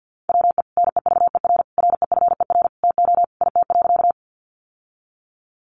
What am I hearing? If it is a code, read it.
Morse code, heard as a keyed tone, about 50 wpm, 700 Hz. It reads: WE NEVER REFER TO IT.
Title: Sentences from Top 1000 Words 50wpm